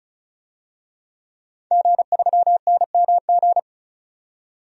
Morse code, heard as a keyed tone, about 35 wpm, 700 Hz. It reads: G3NMG
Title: DXCC-100 Call Signs - 5 Characters - 35wpm